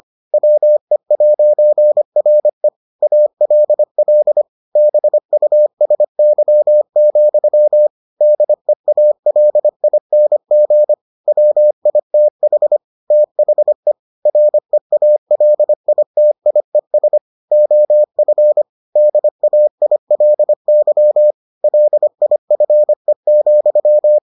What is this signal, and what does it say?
Morse code, heard as a keyed tone, about 25 wpm, 600 Hz. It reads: WE'RE ALL BUSY, DEALING WITH THE REALITIES OF DAILY LIFE,